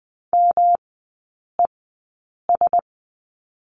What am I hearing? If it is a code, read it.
Morse code, heard as a keyed tone, about 20 wpm, spaced as for 8 wpm, 700 Hz. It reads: MES
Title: 107 Lesson 18e - Letter m - Three Characters 20wpm